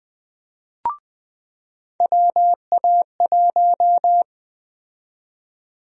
Morse code, heard as a keyed tone, about 20 wpm, 700 Hz. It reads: WA1
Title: USA Call Sign Prefixes - 20wpm